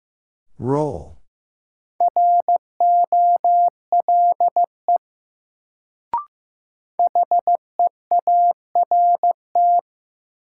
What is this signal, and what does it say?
Morse code, heard as a keyed tone, about 16 wpm, 700 Hz. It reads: ROLE HEART
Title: Sets of 3 Words - New Word Drill - 15wpm